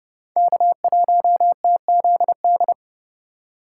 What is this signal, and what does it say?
Morse code, heard as a keyed tone, about 28 wpm, 700 Hz. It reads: K1TZD